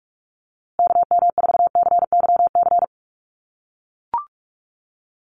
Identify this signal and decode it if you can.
Morse code, heard as a keyed tone, about 44 wpm, 700 Hz. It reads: KM4CYC
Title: USA Call Signs Repeated 3x 45wpm